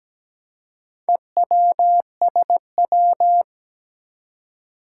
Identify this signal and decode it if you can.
Morse code, heard as a keyed tone, about 17 wpm, 700 Hz. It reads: EWSW
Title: Four Challenging Letter-Number Combinations 17wpm